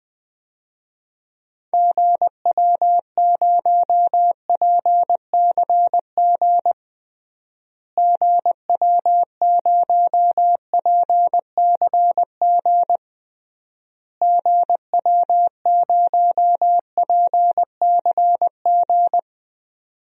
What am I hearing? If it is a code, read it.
Morse code, heard as a keyed tone, about 20 wpm, 700 Hz. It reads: GW0PCG GW0PCG GW0PCG
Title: DXCC-100 Call Signs - 6 Characters - Repeated 3x - 20wpm